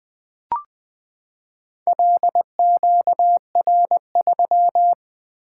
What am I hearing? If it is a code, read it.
Morse code, heard as a keyed tone, about 20 wpm, 700 Hz. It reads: LQR3
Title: Four Letter-Number Combinations - 20wpm